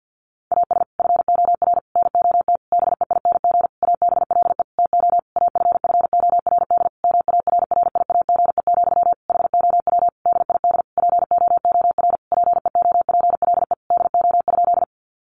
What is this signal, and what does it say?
Morse code, heard as a keyed tone, about 50 wpm, 700 Hz. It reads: AS FOR NOT BEING ABLE TO AFFORD MARRIAGE, HOW DID POOR PEOPLE DO?
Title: Sentences from Top 1100 Words 50wpm